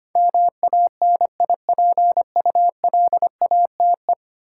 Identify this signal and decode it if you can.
Morse code, heard as a keyed tone, about 25 wpm, 700 Hz. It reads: MANIPULATE
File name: 10 Letter Words - First 3 Letters Given - 25wpm